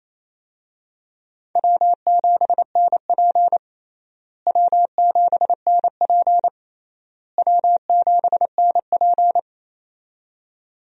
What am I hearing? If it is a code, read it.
Morse code, heard as a keyed tone, about 28 wpm, 700 Hz. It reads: W7NP W7NP W7NP
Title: USA Call Signs - 4 Characters - Repeated 3x - 28wpm